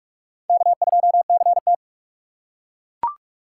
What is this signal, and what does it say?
Morse code, heard as a keyed tone, about 42 wpm, 700 Hz. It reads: K2CT
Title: USA Call Signs - 4 Characters - 4x Character Spacing - 45wpm